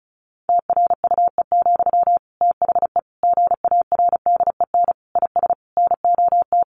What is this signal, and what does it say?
Morse code, heard as a keyed tone, about 34 wpm, 700 Hz. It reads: TRUE, THE GARDEN IS NOT
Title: Sentences from Top 700 Words 35wpm